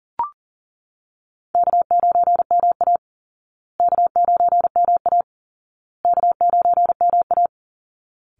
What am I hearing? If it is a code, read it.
Morse code, heard as a keyed tone, about 40 wpm, 700 Hz. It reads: K9MA K9MA K9MA
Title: Easy Call Signs Repeated 3x - 40wpm